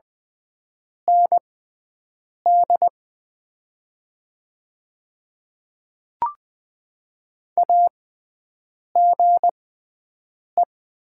Morse code, heard as a keyed tone, about 20 wpm, spaced as for 7 wpm, 690 Hz. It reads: ND AGE
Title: 6x Character Spacing - Top 500 Words - 20wpm